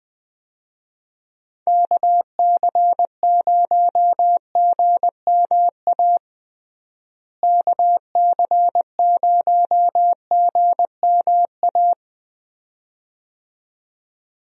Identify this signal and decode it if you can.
Morse code, heard as a keyed tone, about 21 wpm, 700 Hz. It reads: KC0GMA KC0GMA
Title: USA Call Signs - 6 Characters - Repeated 2x - 20wpm